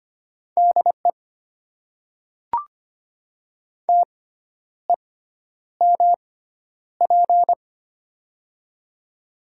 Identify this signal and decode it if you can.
Morse code, heard as a keyed tone, about 25 wpm, spaced as for 10 wpm, 700 Hz. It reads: DE TEMP
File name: Top 90 QSO Elements - 6x Character Spacing - 25wpm